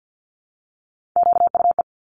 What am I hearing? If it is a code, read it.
Morse code, heard as a keyed tone, about 50 wpm, 700 Hz. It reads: QUE
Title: Top Three Letter Combinations in Words 50wpm